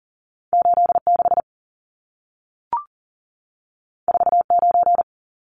Morse code, heard as a keyed tone, about 40 wpm, 700 Hz. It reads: 86 49
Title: Numbers - 2-Digit 40wpm